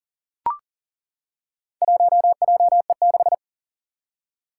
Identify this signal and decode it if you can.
Morse code, heard as a keyed tone, about 37 wpm, 700 Hz. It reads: 1JE6
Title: Four Challenging Letter-Number Combinations 40wpm